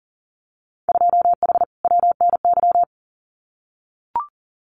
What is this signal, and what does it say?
Morse code, heard as a keyed tone, about 40 wpm, 700 Hz. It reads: 2H WNY
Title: Winter Field Day Exchanges Encore 40wpm